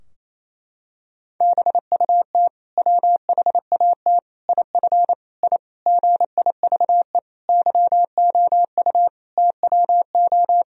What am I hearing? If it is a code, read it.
Morse code, heard as a keyed tone, about 27 wpm, 700 Hz. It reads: BUT WHAT IF I GIVE YOU TWO
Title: Sentences from Top 1000 Words - Review Part 2 - 28wpm